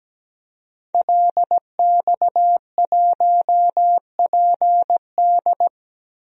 Morse code, heard as a keyed tone, about 17 wpm, 700 Hz. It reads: LX1PD
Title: DXCC-100 Call Signs - 5 Characters - Repeated 3x - 17wpm